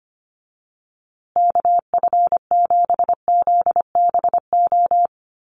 Morse code, heard as a keyed tone, about 25 wpm, 700 Hz. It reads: KF7ZBO
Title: USA Call Signs Repeated 3x 25wpm